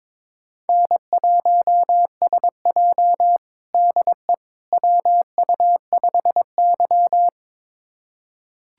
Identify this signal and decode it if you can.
Morse code, heard as a keyed tone, about 22 wpm, 700 Hz. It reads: N1SJ DE WU5Y
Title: USA Callsign DE Callsign - 4 Characters - 22wpm